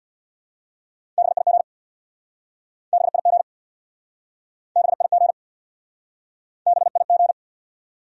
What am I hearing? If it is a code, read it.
Morse code, heard as a keyed tone, about 69 wpm, 700 Hz. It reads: BIG BIG BIG BIG